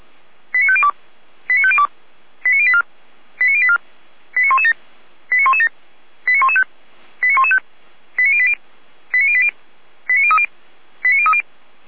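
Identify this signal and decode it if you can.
Five-tone selective call (ZVEI), heard as quick runs of five tones, tones of about 70 ms. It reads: 89502 89502 89084 89084 89117 89117 89115 89115 89080 89080 89030 89030
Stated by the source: ZVEI five-tone selective call received by SDR.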